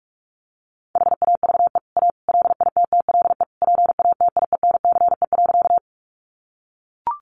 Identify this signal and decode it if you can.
Morse code, heard as a keyed tone, about 44 wpm, 700 Hz. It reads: HAVE A LITTLE PATIENCE.
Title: Sentences from Top 1300 Words 45wpm